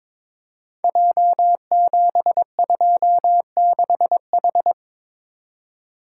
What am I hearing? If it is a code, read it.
Morse code, heard as a keyed tone, about 22 wpm, 700 Hz. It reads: J726H
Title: Code Groups - 5 Letter:Numbers - 22wpm